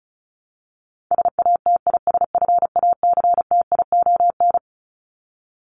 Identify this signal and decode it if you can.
Morse code, heard as a keyed tone, about 34 wpm, 700 Hz. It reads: SATISFACTION
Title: Top 1751-2000 Words 35wpm